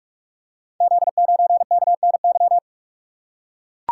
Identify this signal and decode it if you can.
Morse code, heard as a keyed tone, about 44 wpm, 700 Hz. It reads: Z9KNY
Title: Code Groups - 5 Letter:Numbers - 45wpm